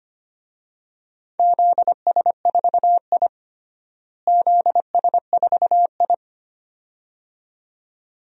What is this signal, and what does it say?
Morse code, heard as a keyed tone, about 25 wpm, 700 Hz. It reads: ZS4I ZS4I
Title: DXCC-100 Call Signs - 4 Characters - Repeated 2x - 25wpm